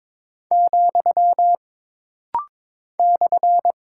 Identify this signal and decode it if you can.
Morse code, heard as a keyed tone, about 22 wpm, 700 Hz. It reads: , /
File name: practice / 22wpm / Single Punctuation-Prosign - Rapid-Fire - 22wpm